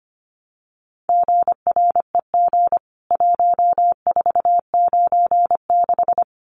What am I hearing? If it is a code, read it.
Morse code, heard as a keyed tone, about 25 wpm, 700 Hz. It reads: GREG 1496
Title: CWT Exchange 25wpm